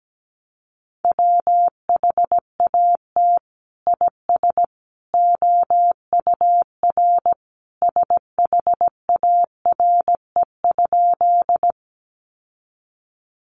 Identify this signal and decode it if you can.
Morse code, heard as a keyed tone, about 17 wpm, 700 Hz. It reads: WHAT IS OUR SHARE?
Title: Sentences from Top 900 Words 17wpm